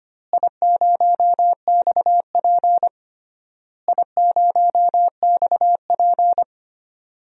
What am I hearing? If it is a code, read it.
Morse code, heard as a keyed tone, about 25 wpm, 700 Hz. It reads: I0XP I0XP
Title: DXCC-100 Call Signs - 4 Characters - Repeated 3x - 25wpm